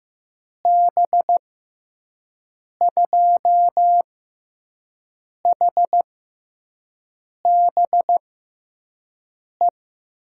Morse code, heard as a keyed tone, about 15 wpm, spaced as for 5 wpm, 700 Hz. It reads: B2HBE
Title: USA Call Signs - 6 Characters - 6x Character Spacing - 15wpm